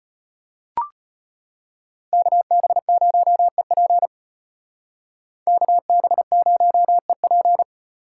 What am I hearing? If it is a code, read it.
Morse code, heard as a keyed tone, about 32 wpm, 700 Hz. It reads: KB0EP KB0EP